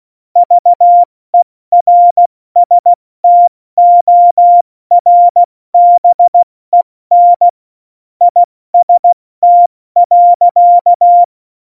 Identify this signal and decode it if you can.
Morse code, heard as a keyed tone, about 16 wpm, 700 Hz. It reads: VERSTORBEN IST.